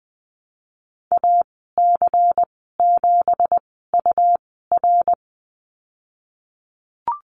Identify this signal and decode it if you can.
Morse code, heard as a keyed tone, about 20 wpm, spaced as for 15 wpm, 700 Hz. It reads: AC7UR
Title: 173 Lesson 28g - Number 9 - Call Signs 20wpm